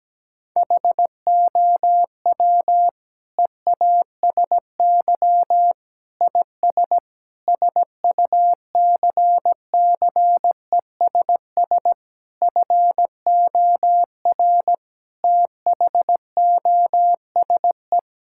Morse code, heard as a keyed tone, about 17 wpm, 700 Hz. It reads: HOW EASY IS SUCCESS FOR THOSE
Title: Sentences from Top 1000 Words - Review Part 6 - 17wpm